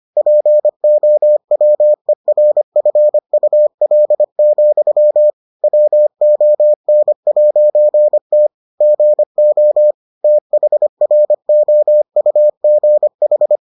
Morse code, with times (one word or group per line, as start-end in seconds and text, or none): 0.2-5.3 POWERFUL,
5.6-8.5 WON'T
8.8-9.9 GO
10.2-13.6 THROUGH